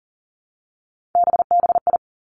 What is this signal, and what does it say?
Morse code, heard as a keyed tone, about 38 wpm, 700 Hz. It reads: BBI
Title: Three Challenging Letter-Number Combinations 40wpm